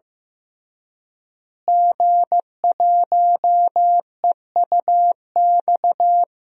Morse code, heard as a keyed tone, about 15 wpm, 700 Hz. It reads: G1EUX
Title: DXCC-100 Call Signs - 5 Characters - Repeated 2x - 15wpm